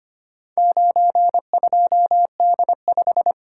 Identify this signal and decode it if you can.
Morse code, heard as a keyed tone, about 25 wpm, 700 Hz. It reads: 92D5
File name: practice / 25wpm / Four Letter-Number Combinations - 25wpm